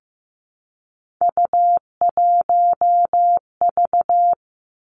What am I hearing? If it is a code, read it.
Morse code, heard as a keyed tone, about 15 wpm, 700 Hz. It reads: U1V
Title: Three Challenging Letter-Number Combinations 15wpm